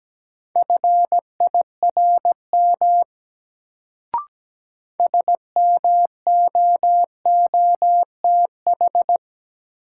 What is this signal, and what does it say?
Morse code, heard as a keyed tone, about 17 wpm, 700 Hz. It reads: FIRM SMOOTH
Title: Sentences from Top 1400 Words - New Word Drill - 17wpm